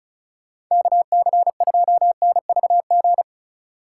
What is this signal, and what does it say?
Morse code, heard as a keyed tone, about 34 wpm, 700 Hz. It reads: KC2NVG
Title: USA Call Signs - 6 Characters - 2x Character Spacing - 35wpm